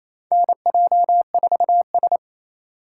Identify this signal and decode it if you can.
Morse code, heard as a keyed tone, about 28 wpm, 700 Hz. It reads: NJ4S
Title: USA Call Signs - 4 Characters - 3x Character Spacing - 28wpm